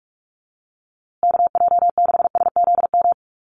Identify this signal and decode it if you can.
Morse code, heard as a keyed tone, about 44 wpm, 700 Hz. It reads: KJ6SZM
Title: USA Call Signs Repeated 2x 45wpm